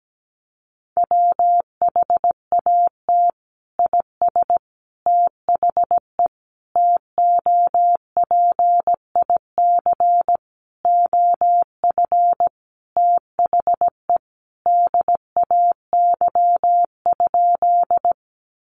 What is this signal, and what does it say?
Morse code, heard as a keyed tone, about 17 wpm, 700 Hz. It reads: WHAT IS THE TOPIC OF THE DAY?
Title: Sentences from Top 1200 Words 17wpm